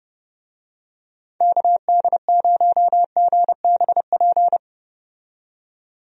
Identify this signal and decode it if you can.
Morse code, heard as a keyed tone, about 31 wpm, 700 Hz. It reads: KD0GBP
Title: USA Call Signs - 6 Characters - 30wpm